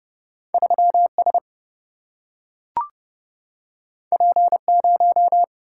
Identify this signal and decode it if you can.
Morse code, heard as a keyed tone, about 30 wpm, 700 Hz. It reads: 3 S P 0
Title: Single-Letter Single-Number 30wpm